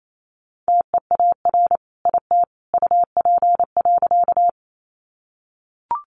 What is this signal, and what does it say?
Morse code, heard as a keyed tone, about 28 wpm, 700 Hz. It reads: TEAR IT UP.
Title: A Sound Of Thunder 28wpm